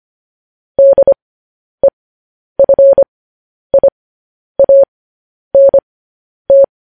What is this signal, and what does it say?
Morse code, heard as a keyed tone, about 25 wpm, spaced as for 10 wpm, 550 Hz. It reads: DEFIANT
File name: podcast / News headlines-1x-25x10